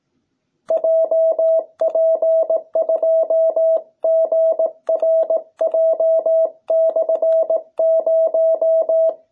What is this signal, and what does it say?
Morse code, heard as a keyed tone, about 18 wpm, 660 Hz. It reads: JP2GRJ/0